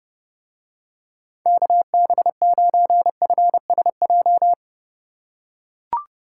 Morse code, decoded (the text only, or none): KB9FSJ